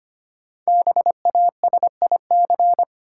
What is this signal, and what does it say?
Morse code, heard as a keyed tone, about 25 wpm, 700 Hz. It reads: BASIC
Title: Sentences from Top 1000 Words - Review Part 3 - New Word Drill - 25wpm